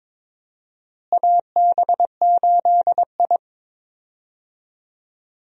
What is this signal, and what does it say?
Morse code, heard as a keyed tone, about 22 wpm, 700 Hz. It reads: AB8I
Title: USA Call Signs - 4 Characters - Repeated 2x - 22wpm